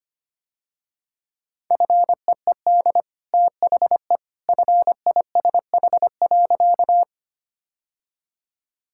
Morse code, heard as a keyed tone, about 25 wpm, 700 Hz. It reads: FEED THE FISH.
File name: Easy Sentences with Hints - 25wpm